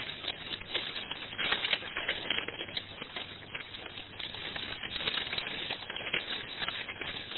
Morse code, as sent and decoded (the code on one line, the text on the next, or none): none